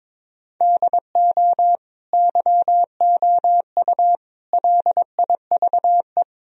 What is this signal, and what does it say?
Morse code, heard as a keyed tone, about 22 wpm, 700 Hz. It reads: DO YOU LIVE
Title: Sentences from Top 1000 Words - Review Part 3 - 22wpm